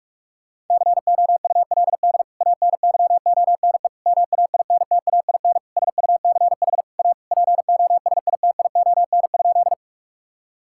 Morse code, heard as a keyed tone, about 46 wpm, 700 Hz. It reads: COULD ANYONE MAINTAIN SUCH A POSITION?
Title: Sentences from Top 1000 Words - Review Part 5 - 45wpm